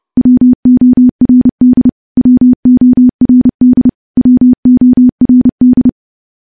Morse code, heard as a keyed tone, about 30 wpm, 250 Hz. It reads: WORD WORD WORD
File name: morse / morse-training-30WPM